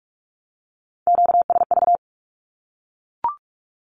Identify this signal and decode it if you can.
Morse code, heard as a keyed tone, about 44 wpm, 700 Hz. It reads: QSV
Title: Q-Codes 45wpm